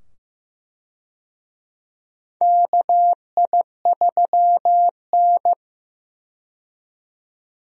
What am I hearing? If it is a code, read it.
Morse code, heard as a keyed tone, about 15 wpm, 700 Hz. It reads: KI3N
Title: USA Call Signs - 4 Characters - 15wpm